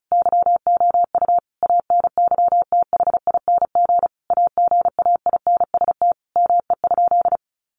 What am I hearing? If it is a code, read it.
Morse code, heard as a keyed tone, about 34 wpm, 700 Hz. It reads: YOU ANYTHING AGAINST ME?
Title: Sentences from Top 1000 Words 35wpm